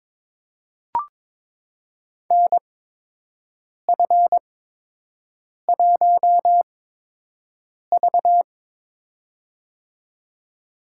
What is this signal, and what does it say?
Morse code, heard as a keyed tone, about 22 wpm, spaced as for 6 wpm, 700 Hz. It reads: NF1V
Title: USA Call Signs - 8x Character Spacing - 22wpm